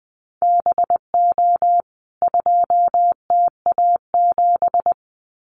Morse code, heard as a keyed tone, about 20 wpm, 700 Hz. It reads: BO 2TA7
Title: CWT Exchange 20wpm